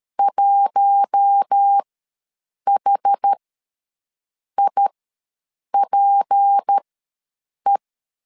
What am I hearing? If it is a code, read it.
Morse code, heard as a keyed tone, about 13 wpm, spaced as for 7 wpm, 800 Hz. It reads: 1HIPE
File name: LEZIONI / 6 PKX0 / 6 PK ws 12wpm 3116